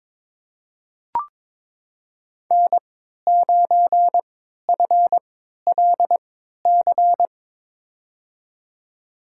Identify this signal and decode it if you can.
Morse code, heard as a keyed tone, about 22 wpm, spaced as for 12 wpm, 700 Hz. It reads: N9FLC
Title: USA Call Signs - 5 Characters - 3x Character Spacing - 22wpm